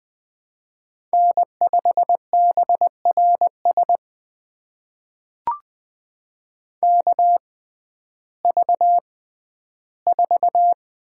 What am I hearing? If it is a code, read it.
Morse code, heard as a keyed tone, about 21 wpm, spaced as for 12 wpm, 700 Hz. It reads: N5BRS KV4
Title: USA Call Signs - 5 Characters - 6x Character Spacing - 20wpm